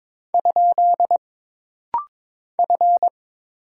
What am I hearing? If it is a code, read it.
Morse code, heard as a keyed tone, about 22 wpm, 700 Hz. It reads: ? F
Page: This practice set is composed of Letters, Numbers, common punctuations, and common prosigns.